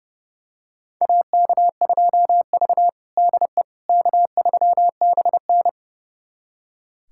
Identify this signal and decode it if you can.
Morse code, heard as a keyed tone, about 32 wpm, 700 Hz. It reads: AK2V DE K3BN
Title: USA Callsign DE Callsign - 4 Characters - 30wpm